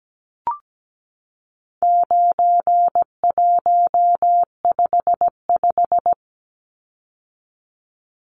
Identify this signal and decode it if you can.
Morse code, heard as a keyed tone, about 17 wpm, 700 Hz. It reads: 9155
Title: Numbers - 4-Digit 17wpm